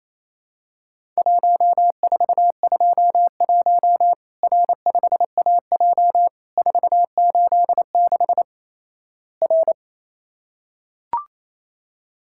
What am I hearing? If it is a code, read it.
Morse code, heard as a keyed tone, about 28 wpm, 700 Hz. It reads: 1421 R5AJ 486 R
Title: Worked All Europe DX Contest - QTC Exchange - 28wpm